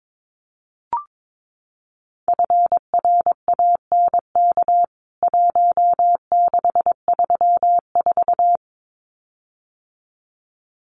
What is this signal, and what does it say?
Morse code, heard as a keyed tone, about 22 wpm, 700 Hz. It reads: FRANK 1634
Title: CWT Exchange 22wpm